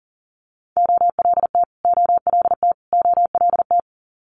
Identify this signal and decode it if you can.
Morse code, heard as a keyed tone, about 40 wpm, 700 Hz. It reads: OLT OLT OLT
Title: Multi-letter Phonemes Repeated 3x 40wpm